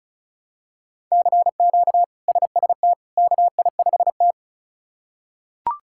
Morse code, heard as a keyed tone, about 36 wpm, 700 Hz. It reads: CQ SST KI5T
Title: K1USN SST QSO Practice - 35wpm